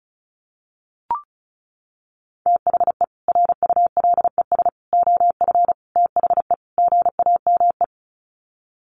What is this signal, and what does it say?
Morse code, heard as a keyed tone, about 34 wpm, 700 Hz. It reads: THE RULES OF THE GAME
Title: Sets of 5 Words Encore2 35wpm